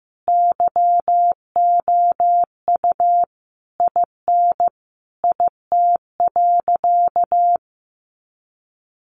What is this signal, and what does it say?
Morse code, heard as a keyed tone, about 15 wpm, 700 Hz. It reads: YOU IN IT.
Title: Sentences from Top 1500 Words 15wpm